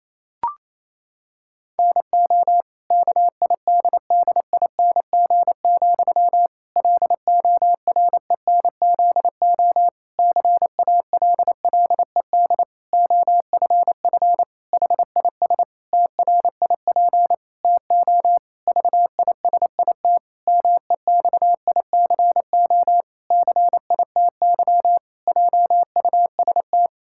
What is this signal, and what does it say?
Morse code, heard as a keyed tone, about 28 wpm, 700 Hz. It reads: NO KIDDING, LORENZO CALLED OFF HIS TRIP TO VISIT MEXICO CITY JUST
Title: Pangram Challenge v2 - 28wpm